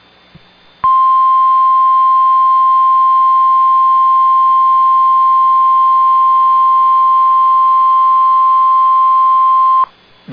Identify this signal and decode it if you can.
Emergency Alert System (SAME) tones (the 1050 Hz attention tone). It is heard from 1.0 to 9.6 s.